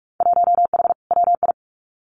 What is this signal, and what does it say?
Morse code, heard as a keyed tone, about 44 wpm, 700 Hz. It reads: 1H WI
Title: Winter Field Day Exchanges 45wpm